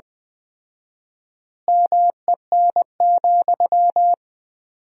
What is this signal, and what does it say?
Morse code, heard as a keyed tone, about 20 wpm, 700 Hz. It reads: MEN,
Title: 4x Word-Spacing - Sentences from Top 500 Words - 20wpm